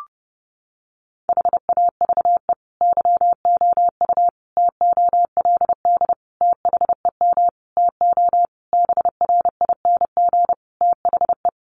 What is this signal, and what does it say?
Morse code, heard as a keyed tone, about 28 wpm, 700 Hz. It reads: HAVE YOU TOLD THEM TO BRING THE
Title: Sentences from Top 500 Words 30wpm